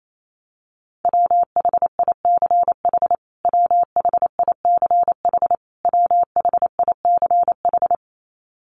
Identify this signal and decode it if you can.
Morse code, heard as a keyed tone, about 28 wpm, 700 Hz. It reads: WHICH WHICH WHICH
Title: Top 200 Words Repeated 28wpm